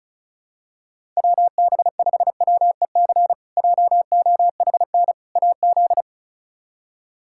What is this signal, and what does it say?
Morse code, heard as a keyed tone, about 34 wpm, 700 Hz. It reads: WB5WEC JOHN AZ
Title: K1USN SST QSO Practice - 35wpm